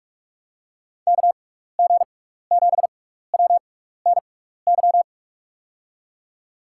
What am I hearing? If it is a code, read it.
Morse code, heard as a keyed tone, about 46 wpm, spaced as for 15 wpm, 700 Hz. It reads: KG7WNY
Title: USA Call Signs - 6x Character Spacing - 45wpm